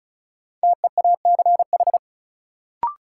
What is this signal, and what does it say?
Morse code, heard as a keyed tone, about 36 wpm, 700 Hz. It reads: TEACH
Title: Sentences from Top 500 Words - New Word Drill - 35wpm